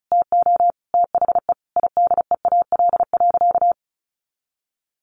Morse code, heard as a keyed tone, about 34 wpm, 700 Hz. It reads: TO THE IDEAL.
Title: Sentences from Top 1400 Words 35wpm